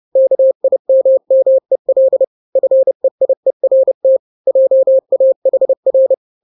Morse code, heard as a keyed tone, about 30 wpm, 530 Hz. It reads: KIMMEL FEIERT JAHR
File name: 29wpm